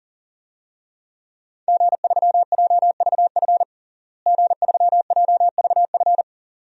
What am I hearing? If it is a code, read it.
Morse code, heard as a keyed tone, about 37 wpm, 700 Hz. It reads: G3JVF G3JVF
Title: DXCC-100 Call Signs - 5 Characters - Repeated 2x - 40wpm